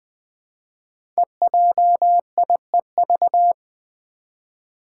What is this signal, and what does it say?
Morse code, heard as a keyed tone, about 20 wpm, 700 Hz. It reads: EJIEV
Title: Five Challenging Letter-Number Combinations 20wpm